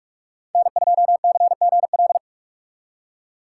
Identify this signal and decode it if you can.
Morse code, heard as a keyed tone, about 53 wpm, 700 Hz. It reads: N2CGL
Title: Speed Racing - USA 1x3 Call Signs 45wpm